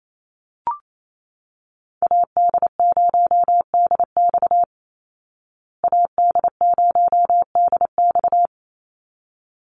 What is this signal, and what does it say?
Morse code, heard as a keyed tone, about 28 wpm, 700 Hz. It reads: AD0DX AD0DX
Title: Call Signs Repeated 3x - 28wpm